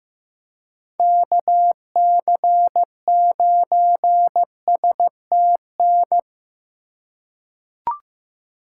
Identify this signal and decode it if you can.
Morse code, heard as a keyed tone, about 15 wpm, 700 Hz. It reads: KC9STN
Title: USA Call Signs - 6 Characters - 2x Character Spacing - 15wpm